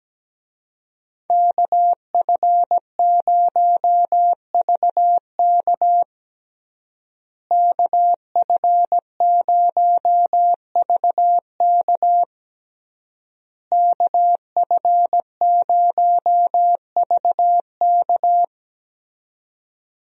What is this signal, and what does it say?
Morse code, heard as a keyed tone, about 17 wpm, 700 Hz. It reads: KF0VK KF0VK KF0VK
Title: USA Call Signs - 5 Characters - Repeated 3x - 17wpm